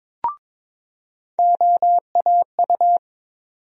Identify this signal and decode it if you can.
Morse code, heard as a keyed tone, about 22 wpm, 700 Hz. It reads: OAU
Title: USA Call Sign Postfixes - 3 Letters 22wpm